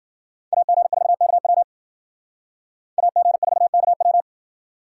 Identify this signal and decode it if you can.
Morse code, heard as a keyed tone, about 52 wpm, 700 Hz. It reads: AG4KW AG4KW